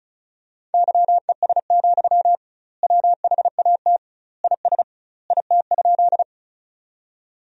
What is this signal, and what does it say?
Morse code, heard as a keyed tone, about 34 wpm, 700 Hz. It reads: YES, WHAT IS IT?
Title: 2x Word-Spacing - Sentences from Top 500 Words - 35wpm